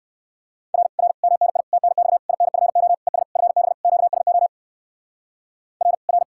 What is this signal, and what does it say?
Morse code, heard as a keyed tone, about 59 wpm, 700 Hz. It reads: R R MNI TNX INFO ES FB QSO R R